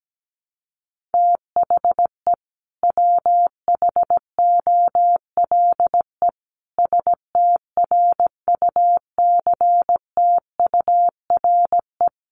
Morse code, heard as a keyed tone, about 17 wpm, 700 Hz. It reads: THE WHOLE STRUCTURE
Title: Sentences from Top 1500 Words 17wpm